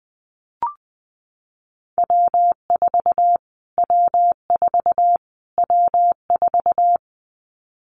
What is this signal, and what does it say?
Morse code, heard as a keyed tone, about 20 wpm, 700 Hz. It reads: W4 W4 W4
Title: Callsign Prefixes for Top 25 DXCC - Repeated 3x - 20wpm